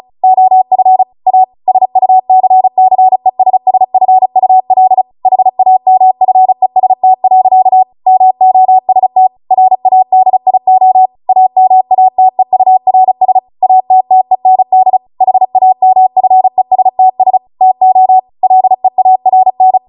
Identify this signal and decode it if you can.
Morse code, heard as a keyed tone, about 36 wpm, 750 Hz. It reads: OF A SUCCESSFUL HAMFEST. MOST RADIO AMATEURS ATTEND HAMFESTS TO LEARN
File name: Morse / 35WPM